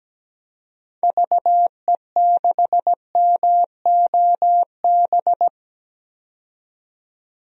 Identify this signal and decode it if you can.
Morse code, heard as a keyed tone, about 17 wpm, 700 Hz. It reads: VE6MOB
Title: DXCC-100 Call Signs - 6 Characters - 17wpm